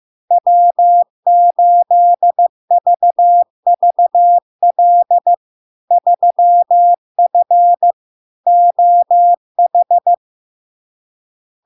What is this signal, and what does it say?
Morse code, heard as a keyed tone, about 15 wpm, 700 Hz. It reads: W8VVL 3F OH
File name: LARK-FD-2022-Log-in-CW-15wpm-700-Hz